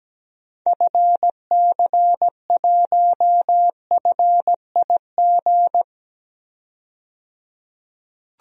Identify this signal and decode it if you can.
Morse code, heard as a keyed tone, about 17 wpm, 700 Hz. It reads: FC1FIG
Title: DXCC-100 Call Signs - 6 Characters - 17wpm